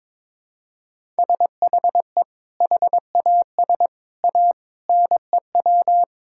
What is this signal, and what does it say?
Morse code, heard as a keyed tone, about 22 wpm, 700 Hz. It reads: SHE HAS A NEW
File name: practice / 22wpm / Sentences with 2-Word Phrases - 22wpm